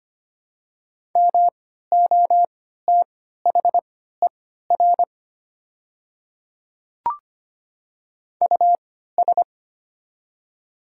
Morse code, heard as a keyed tone, about 25 wpm, spaced as for 14 wpm, 700 Hz. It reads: MOTHER US
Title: 3x Character Spacing - Top 200 Words - 25wpm